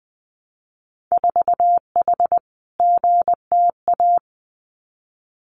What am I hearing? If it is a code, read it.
Morse code, heard as a keyed tone, about 20 wpm, 700 Hz. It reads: 4H GTA
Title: Winter Field Day Exchanges Encore 20wpm